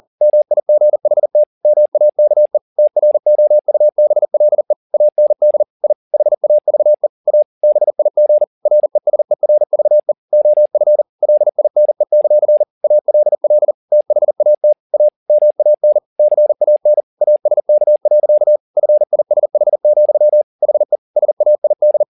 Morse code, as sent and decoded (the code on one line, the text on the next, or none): -- .. --. .... - / -- .- -.- . / - .-. --- ..- -... .-.. / .- -. -.. / .. / .... .- ...- . / .- / -... .. --. / .-. ... .-. ...- . / --- ..-. / .-.. .. -. -.-.-. / .- .-.. .-.. / - .... .- - / .- / -- .- -. / -.-. .- -. / .- ... -.- .-.-.- / ..-. .. ... .... --..-- / .... / ... .- .. -..
MIGHT MAKE TROUBLE AND I HAVE A BIG RESERVE OF LINE; ALL THAT A MAN CAN ASK. FISH, HE SAID